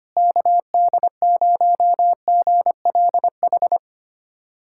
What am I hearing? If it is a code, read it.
Morse code, heard as a keyed tone, about 25 wpm, 700 Hz. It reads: KD0GLH
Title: USA Callsign DE Callsign 25wpm